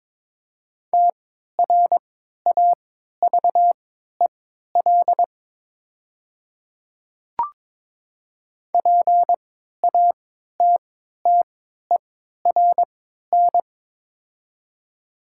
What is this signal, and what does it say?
Morse code, heard as a keyed tone, about 23 wpm, spaced as for 13 wpm, 700 Hz. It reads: TRAVEL PATTERN
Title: 3x Character Spacing - Top 500 Words - 22wpm